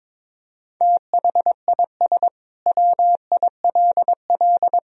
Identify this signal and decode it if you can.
Morse code, heard as a keyed tone, about 22 wpm, 700 Hz. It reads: THIS WILL
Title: Sentences from Top 400 Words and Abbreviations 22wpm